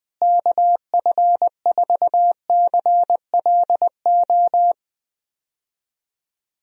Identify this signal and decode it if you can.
Morse code, heard as a keyed tone, about 20 wpm, 700 Hz. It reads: KF4CLO
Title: USA Call Signs - 6 Characters - Repeated 3x - 20wpm